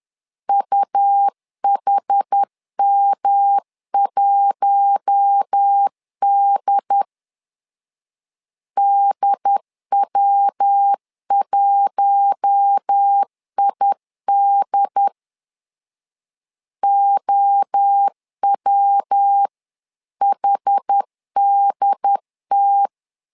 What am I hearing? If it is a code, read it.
Morse code, heard as a keyed tone, about 10 wpm, 800 Hz. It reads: UHM1D DW1ID OWHDT
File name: LEZIONI / 4 Q1 10wpm 2448